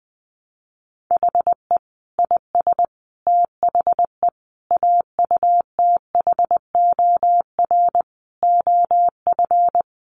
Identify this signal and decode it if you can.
Morse code, heard as a keyed tone, about 20 wpm, 700 Hz. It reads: HE IS THE AUTHOR OF
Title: Sets of 5 Words Encore2 20wpm